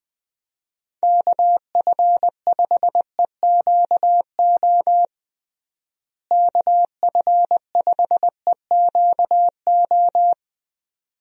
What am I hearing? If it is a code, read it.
Morse code, heard as a keyed tone, about 21 wpm, 700 Hz. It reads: KF5EQO KF5EQO
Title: USA Call Signs - 6 Characters - Repeated 2x - 20wpm